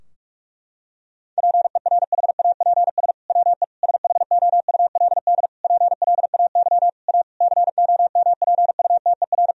Morse code, heard as a keyed tone, about 46 wpm, 700 Hz. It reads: PERHAPS WE SHOULD PLAY A COMPUTER
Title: Sentences from Top 1000 Words - Review Part 3 - 45wpm